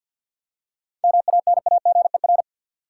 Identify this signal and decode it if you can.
Morse code, heard as a keyed tone, about 52 wpm, 700 Hz. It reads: MANAGER
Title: Seven Letter Words - 50wpm